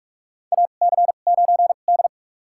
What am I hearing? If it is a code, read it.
Morse code, heard as a keyed tone, about 42 wpm, spaced as for 31 wpm, 700 Hz. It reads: AC9D
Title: USA Call Signs - 4 Characters - 2x Character Spacing - 45wpm